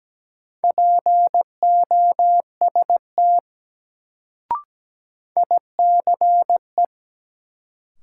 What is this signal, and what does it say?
Morse code, heard as a keyed tone, about 17 wpm, 700 Hz. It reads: POST ICE
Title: Sentences from Top 1000 Words - New Word Drill - 17wpm